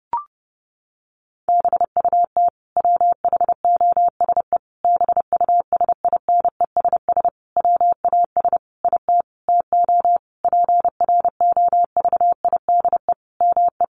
Morse code, heard as a keyed tone, about 30 wpm, 700 Hz. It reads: BUT WHOSE BUSINESS WAS IT TO PROVIDE ME
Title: Sentences from Top 1500 Words 30wpm